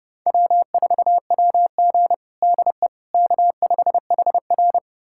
Morse code, heard as a keyed tone, about 30 wpm, 700 Hz. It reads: W4WG DE K5HR
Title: USA Callsign DE Callsign - 4 Characters - 30wpm